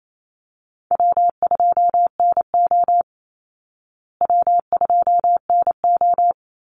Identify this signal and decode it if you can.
Morse code, heard as a keyed tone, about 27 wpm, 700 Hz. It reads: W2NO W2NO
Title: Easy Call Signs Repeated 2x - 28wpm